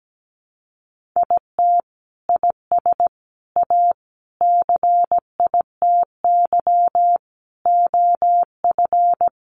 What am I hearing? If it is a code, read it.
Morse code, heard as a keyed tone, about 17 wpm, 700 Hz. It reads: IT IS A CITY OF
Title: Sentences from Top 1500 Words 17wpm